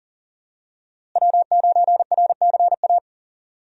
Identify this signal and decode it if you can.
Morse code, heard as a keyed tone, about 40 wpm, 700 Hz. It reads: W9RCA
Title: USA Call Signs - 5 Characters - Repeated 2x - 40wpm